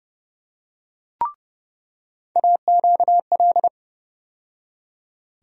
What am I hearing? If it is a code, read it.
Morse code, heard as a keyed tone, about 30 wpm, 700 Hz. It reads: AQL
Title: USA Call Sign Postfixes - 3 Letters 30wpm